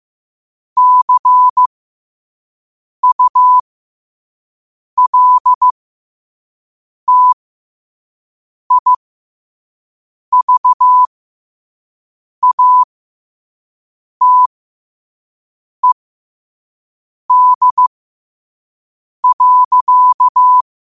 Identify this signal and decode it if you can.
Morse code, heard as a keyed tone, about 15 wpm, spaced as for 5 wpm, 1000 Hz. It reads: CULTIVATED.
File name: qotdfiles / 5wpm